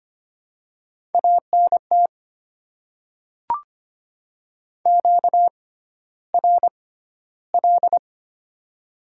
Morse code, heard as a keyed tone, about 25 wpm, spaced as for 13 wpm, 690 Hz. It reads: ANT QRL
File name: Top 90 QSO Elements - 6x Character Spacing - 25wpm